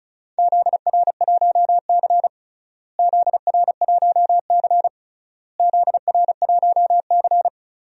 Morse code, heard as a keyed tone, about 34 wpm, 700 Hz. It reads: ZR1C ZR1C ZR1C
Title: DXCC-100 Call Signs - 4 Characters - Repeated 3x - 35wpm